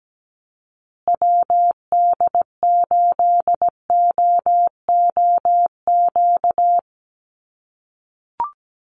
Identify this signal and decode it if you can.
Morse code, heard as a keyed tone, about 17 wpm, 700 Hz. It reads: WD8OOQ
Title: USA Call Signs Repeated 2x 17wpm